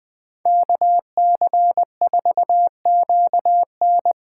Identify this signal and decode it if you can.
Morse code, heard as a keyed tone, about 24 wpm, 700 Hz. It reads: KC4QN
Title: Speed Racing - USA Call Signs 20wpm